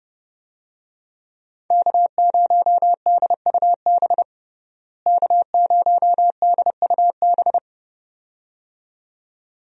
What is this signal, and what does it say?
Morse code, heard as a keyed tone, about 29 wpm, 700 Hz. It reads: K0DUB K0DUB
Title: USA Call Signs - 5 Characters - Repeated 2x - 30wpm